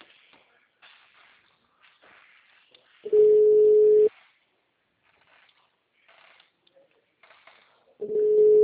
Ringback tone (Europe, 425 Hz).
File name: out- -2035- - - .27